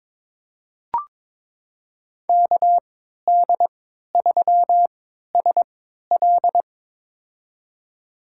Morse code, heard as a keyed tone, about 22 wpm, spaced as for 12 wpm, 700 Hz. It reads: KD3SL
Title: USA Call Signs - 3x Character Spacing - 22wpm